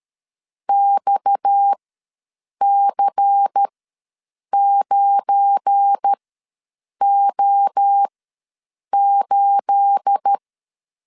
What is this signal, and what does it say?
Morse code, heard as a keyed tone, about 13 wpm, spaced as for 7 wpm, 800 Hz. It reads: XC9O8
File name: LEZIONI / 9 28 ws 12wpm 3354